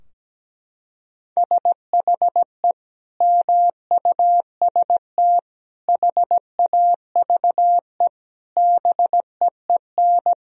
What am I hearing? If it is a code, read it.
Morse code, heard as a keyed tone, about 17 wpm, 700 Hz. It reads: SHE MUST HAVE BEEN
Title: Sentences from Top 1000 Words - Review Part 2 - 17wpm